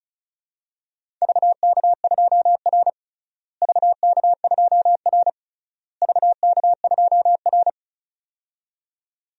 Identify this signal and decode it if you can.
Morse code, heard as a keyed tone, about 34 wpm, 700 Hz. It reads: VK2R VK2R VK2R
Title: DXCC-100 Call Signs - 4 Characters - Repeated 3x - 35wpm